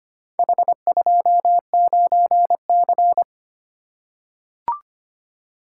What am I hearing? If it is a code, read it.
Morse code, heard as a keyed tone, about 25 wpm, 700 Hz. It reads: H29C
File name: practice / 25wpm / Four Letter-Number Combinations - 25wpm